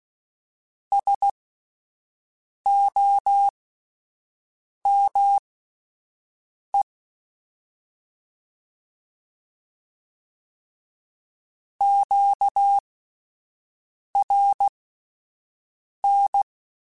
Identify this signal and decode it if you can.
Morse code, heard as a keyed tone, about 16 wpm, spaced as for 5 wpm, 800 Hz. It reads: SOME QRN
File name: (spaced) / QSOs Spaced-mp3s / WD0BGP 1 @10 sp